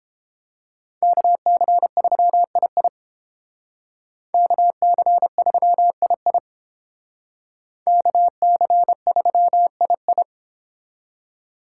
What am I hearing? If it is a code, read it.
Morse code, heard as a keyed tone, about 30 wpm, 700 Hz. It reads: KC3II KC3II KC3II